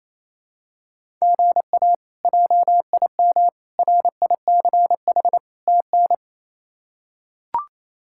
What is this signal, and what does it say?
Morse code, heard as a keyed tone, about 28 wpm, 700 Hz. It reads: GA JIM RICH TN
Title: K1USN SST QSO Practice - 28wpm